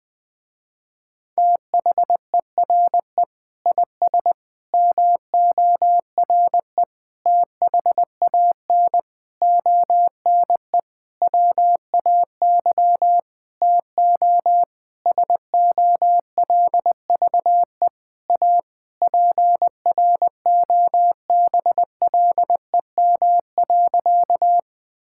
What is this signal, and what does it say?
Morse code, heard as a keyed tone, about 19 wpm, 700 Hz. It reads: THERE IS MORE THAN ONE WAY TO SOLVE A PROBLEM.
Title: Sentences with 9-Word Phrases - Spoken Beginning - 20wpm